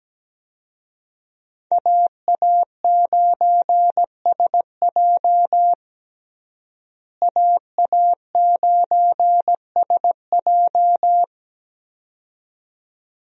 Morse code, heard as a keyed tone, about 17 wpm, 700 Hz. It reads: AA9SJ AA9SJ
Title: USA Call Signs - 5 Characters - Repeated 2x - 17wpm